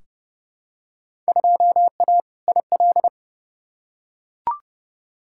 Morse code, heard as a keyed tone, about 29 wpm, 700 Hz. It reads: 2A IL
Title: Field Day Exchanges - 30wpm